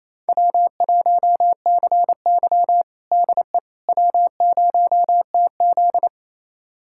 Morse code, heard as a keyed tone, about 28 wpm, 700 Hz. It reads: W1CY DE W0TZ
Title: USA Callsign DE Callsign - 4 Characters - 28wpm